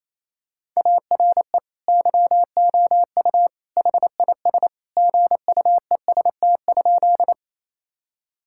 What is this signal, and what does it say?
Morse code, heard as a keyed tone, about 28 wpm, 700 Hz. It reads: ARE YOU HIS GUEST?
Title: Sentences from Top 1600 Words 28wpm